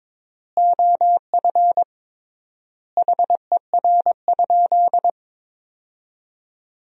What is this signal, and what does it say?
Morse code, heard as a keyed tone, about 22 wpm, 700 Hz. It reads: OF HER?
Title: 3x Word-Spacing - Sentences from Top 200 Words - 22wpm